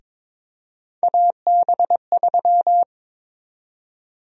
Morse code, heard as a keyed tone, about 22 wpm, 700 Hz. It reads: AB3
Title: USA Call Sign Prefixes - 22wpm